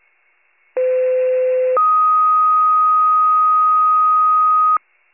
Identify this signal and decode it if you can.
Two-tone page, heard as a one-second tone, then a three-second tone, about 510 then 1250 Hz. It is heard from 0.8 to 4.8 s.